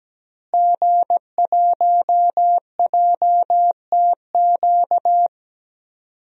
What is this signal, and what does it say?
Morse code, heard as a keyed tone, about 17 wpm, 700 Hz. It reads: G1JTQ
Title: DXCC-100 Call Signs - 5 Characters - Repeated 3x - 17wpm